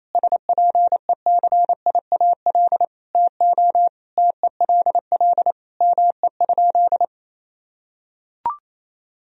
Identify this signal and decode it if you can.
Morse code, heard as a keyed tone, about 27 wpm, 700 Hz. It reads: SPECIAL TO TELL ME?
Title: Sentences from Top 1000 Words - Review Part 3 - 28wpm